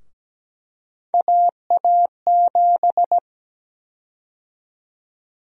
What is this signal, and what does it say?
Morse code, heard as a keyed tone, about 17 wpm, 700 Hz. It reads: AA7
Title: USA Call Sign Prefixes - 17wpm